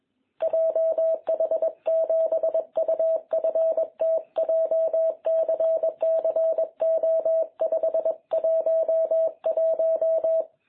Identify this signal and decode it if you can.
Morse code, heard as a keyed tone, about 22 wpm, 660 Hz. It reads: JH7UFT JCCO511
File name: jh8cbh / 2021cw